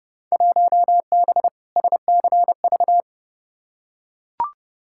Morse code, heard as a keyed tone, about 30 wpm, 700 Hz. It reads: 1B SCV
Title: Field Day Exchanges - 30wpm